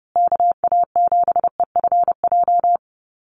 Morse code, heard as a keyed tone, about 30 wpm, 700 Hz. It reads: KA7EFJ
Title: USA Call Signs Repeated 2x 30wpm